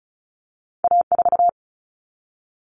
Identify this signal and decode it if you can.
Morse code, heard as a keyed tone, about 36 wpm, 700 Hz. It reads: A4
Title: Common Cut Numbers - 2-Digit 35wpm